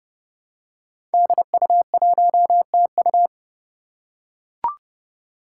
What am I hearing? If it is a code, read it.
Morse code, heard as a keyed tone, about 28 wpm, 700 Hz. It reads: DU1TU
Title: Five Challenging Letter-Number Combinations 30wpm